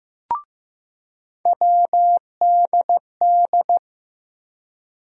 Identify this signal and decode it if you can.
Morse code, heard as a keyed tone, about 16 wpm, 700 Hz. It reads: WDD
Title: USA Call Sign Postfixes - 3 Letters 15wpm